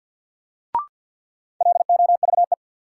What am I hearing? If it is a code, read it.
Morse code, heard as a keyed tone, about 52 wpm, 700 Hz. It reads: LOVE
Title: Sets of 5 Words Encore - New Word Drill - 50wpm